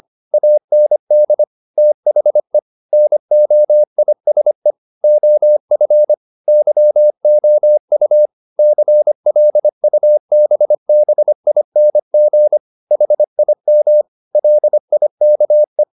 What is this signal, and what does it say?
Morse code, heard as a keyed tone, about 25 wpm, 600 Hz. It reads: AND THE NOISE OF YOU CLUBBING HIM LIKE
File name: the old man and the sea part 1 25wpm